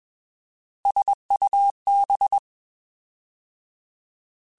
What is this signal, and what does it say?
Morse code, heard as a keyed tone, about 21 wpm, 800 Hz. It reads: SUB
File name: Alphabetical / Alphabetical mp3s / Ss @20